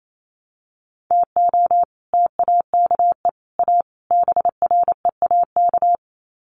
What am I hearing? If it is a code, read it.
Morse code, heard as a keyed tone, about 28 wpm, 700 Hz. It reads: TO TAKE A BREAK
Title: Sets of 4 Words 28wpm